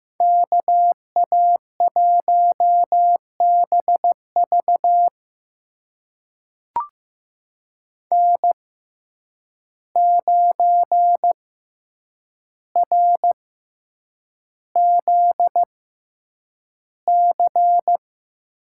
Morse code, heard as a keyed tone, about 15 wpm, spaced as for 8 wpm, 700 Hz. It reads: KA1BV N9RZC
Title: USA Call Signs - 5 Characters - 6x Character Spacing - 15wpm